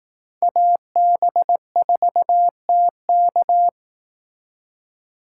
Morse code, heard as a keyed tone, about 18 wpm, 700 Hz. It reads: AB4TK
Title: Speed Racing - USA 2x2 Call Signs 15wpm